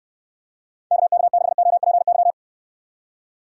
Morse code, heard as a keyed tone, about 66 wpm, 700 Hz. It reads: KK6OPY